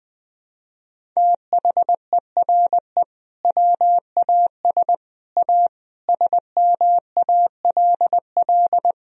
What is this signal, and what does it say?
Morse code, heard as a keyed tone, about 20 wpm, 700 Hz. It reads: THERE WAS A SMALL
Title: Sentences from Top 1000 Words - Review Part 1 - 20wpm